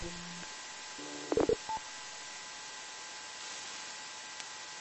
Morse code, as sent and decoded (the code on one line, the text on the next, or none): none